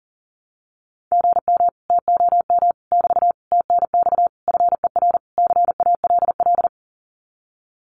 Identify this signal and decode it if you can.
Morse code, heard as a keyed tone, about 40 wpm, 700 Hz. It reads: GM TOM <BT> TNX FER CALL
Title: Common QSO Phrases Encore 40wpm